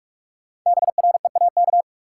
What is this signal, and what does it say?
Morse code, heard as a keyed tone, about 42 wpm, 700 Hz. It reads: BREAK
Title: Sentences from Top 900 Words - New Word Drill - 45wpm